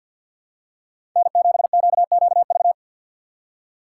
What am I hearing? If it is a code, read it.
Morse code, heard as a keyed tone, about 47 wpm, 700 Hz. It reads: N7QQV
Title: USA Call Signs - 4x Character Spacing - 50wpm